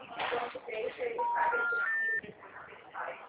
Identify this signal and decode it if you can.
Special information tone, with speech.
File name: out- -2003- - - .49952